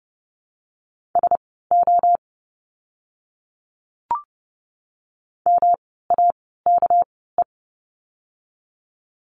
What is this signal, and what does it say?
Morse code, heard as a keyed tone, about 30 wpm, spaced as for 17 wpm, 700 Hz. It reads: SO MAKE
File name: course / 30:10wpm / 230 Lesson 37f - Letter x - Words 30wpm